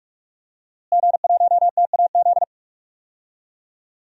Morse code, heard as a keyed tone, about 46 wpm, 700 Hz. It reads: G1TAZ
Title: DXCC-100 Call Signs - 5 Characters - Repeated 2x - 45wpm